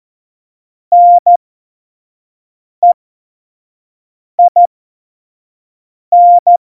Morse code, heard as a keyed tone, about 14 wpm, spaced as for 5 wpm, 700 Hz. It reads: NEIN
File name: Morsekurs / Witze / Witz 19. November 2019, 25 F70